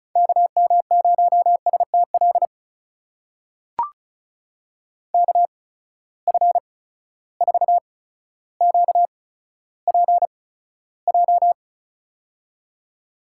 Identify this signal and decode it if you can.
Morse code, heard as a keyed tone, about 36 wpm, spaced as for 15 wpm, 700 Hz. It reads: KM0STL KF4QPJ
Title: USA Call Signs - 6 Characters - 8x Character Spacing - 35wpm